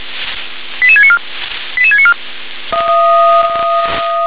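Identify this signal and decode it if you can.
Five-tone selective call (ZVEI), heard as quick runs of five tones, tones of about 70 ms. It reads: 88694 88694
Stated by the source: Five-tone ZVEI selective call 88765, received on SDR.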